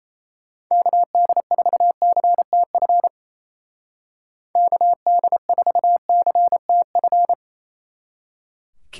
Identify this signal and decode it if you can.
Morse code, heard as a keyed tone, about 34 wpm, 700 Hz. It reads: KD4CTF KD4CTF
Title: Speed Racing - USA 2x3 Call Signs 28wpm